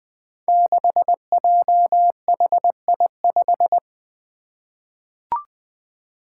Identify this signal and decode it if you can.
Morse code, heard as a keyed tone, about 20 wpm, 700 Hz. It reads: 6JHI5
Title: Five Challenging Letter-Number Combinations 20wpm